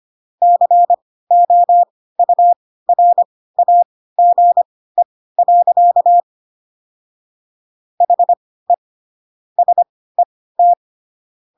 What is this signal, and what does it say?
Morse code, heard as a keyed tone, about 25 wpm, spaced as for 16 wpm, 700 Hz. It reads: COURAGE. HE SET
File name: Pinocchio / Pinocchio ch1 10 25 ik1hge